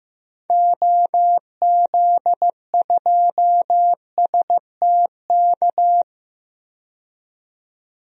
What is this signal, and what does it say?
Morse code, heard as a keyed tone, about 16 wpm, 700 Hz. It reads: OZ2STK
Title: DXCC-100 Call Signs - 6 Characters - 15wpm